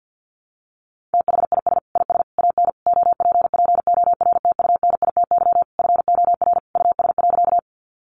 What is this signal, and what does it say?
Morse code, heard as a keyed tone, about 50 wpm, 700 Hz. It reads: THIS IS AN OPPORTUNITY FOR US.
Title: Sentences from Top 1000 Words 50wpm